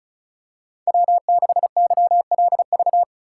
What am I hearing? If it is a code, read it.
Morse code, heard as a keyed tone, about 36 wpm, 700 Hz. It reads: W6YLV
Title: USA Call Signs - 4x Character Spacing - 35wpm